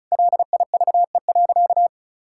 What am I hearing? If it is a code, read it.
Morse code, heard as a keyed tone, about 36 wpm, 700 Hz. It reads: LIVE.
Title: Sentences from Top 1000 Words - Review Part 5 - 35wpm